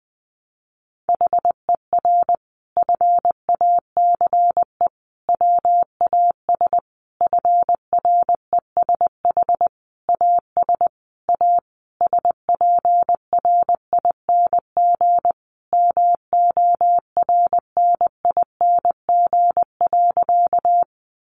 Morse code, heard as a keyed tone, about 20 wpm, 700 Hz. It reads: HER FACE WAS FRESH AS A SPRING MORNING.
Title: Sentences from Top 1200 Words 20wpm